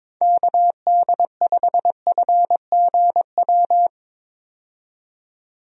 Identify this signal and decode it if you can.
Morse code, heard as a keyed tone, about 22 wpm, 700 Hz. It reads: KD5FGW
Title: USA Call Signs - 6 Characters - 22wpm